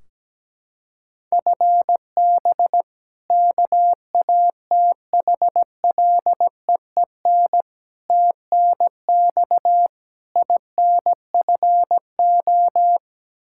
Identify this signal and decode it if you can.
Morse code, heard as a keyed tone, about 17 wpm, 700 Hz. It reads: FB KATHLEEN TNX INFO
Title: Practice QSO - Exchange 3 - More Info - 17wpm